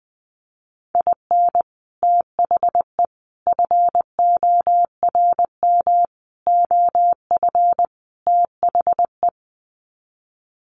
Morse code, heard as a keyed tone, about 19 wpm, 700 Hz. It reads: IN THE FORM OF THE
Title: Sets of 5 Words Encore2 20wpm